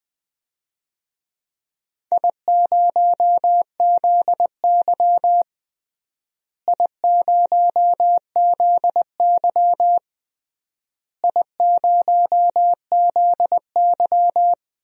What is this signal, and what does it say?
Morse code, heard as a keyed tone, about 20 wpm, 700 Hz. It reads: I0ZY I0ZY I0ZY
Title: DXCC-100 Call Signs - 4 Characters - Repeated 3x - 20wpm